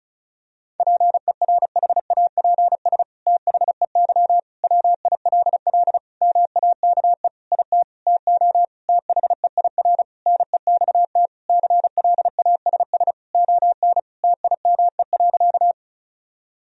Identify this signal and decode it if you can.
Morse code, heard as a keyed tone, about 36 wpm, 700 Hz. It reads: PERHAPS THEY WILL MAKE IT TO THEIR NEXT CLASS ON TIME.
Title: Sentences from Top 1000 Words - Review Part 2 - 35wpm